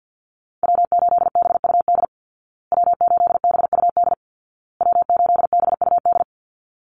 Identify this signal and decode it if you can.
Morse code, heard as a keyed tone, about 50 wpm, 700 Hz. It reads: W8BUD W8BUD W8BUD
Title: Call Signs Repeated 2x - 50wpm